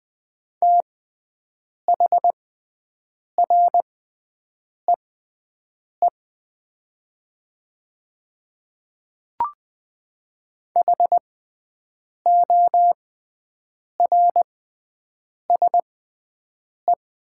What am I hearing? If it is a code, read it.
Morse code, heard as a keyed tone, about 20 wpm, spaced as for 7 wpm, 700 Hz. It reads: THREE HORSE
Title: 6x Character Spacing - Top 300 Words - 20wpm